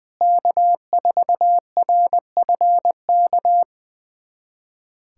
Morse code, heard as a keyed tone, about 20 wpm, 700 Hz. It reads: K4RFK
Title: USA Callsign DE Callsign - 5 Characters - 20wpm